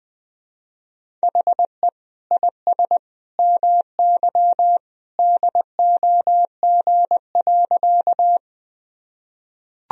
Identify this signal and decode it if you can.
Morse code, heard as a keyed tone, about 21 wpm, 700 Hz. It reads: HE IS MY DOG.
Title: Sentences from Top 1000 Words - Review Part 2 - 20wpm